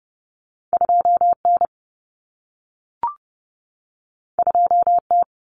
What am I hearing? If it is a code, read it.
Morse code, heard as a keyed tone, about 30 wpm, 700 Hz. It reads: 2N 2T
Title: Common Cut Numbers - 2-Digit 30wpm